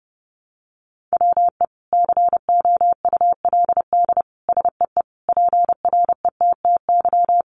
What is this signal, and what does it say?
Morse code, heard as a keyed tone, about 30 wpm, 700 Hz. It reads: WE COULD SEE PRETTY
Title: Sentences from Top 1000 Words 30wpm